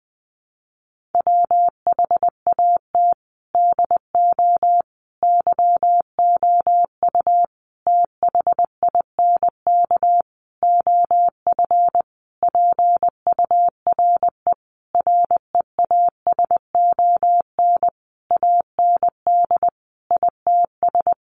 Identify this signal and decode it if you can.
Morse code, heard as a keyed tone, about 20 wpm, 700 Hz. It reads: WHAT DO YOU THINK OF PURE REASON AND ITS
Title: Sentences from Top 1500 Words 20wpm